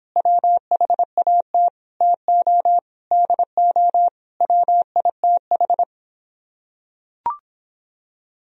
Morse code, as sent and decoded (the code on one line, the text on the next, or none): .-- .... .- - / - --- / -.. --- / .-- .. - ....
WHAT TO DO WITH